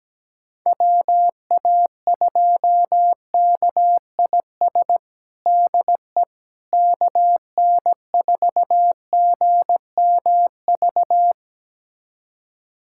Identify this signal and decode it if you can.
Morse code, heard as a keyed tone, about 17 wpm, 700 Hz. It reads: WA2KIS DE KN4GMV
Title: USA Callsign DE Callsign 17wpm